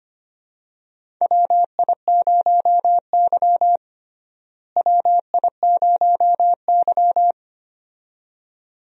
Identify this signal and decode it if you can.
Morse code, heard as a keyed tone, about 25 wpm, 700 Hz. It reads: WI0Y WI0Y
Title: USA Call Signs - 4 Characters - Repeated 2x - 25wpm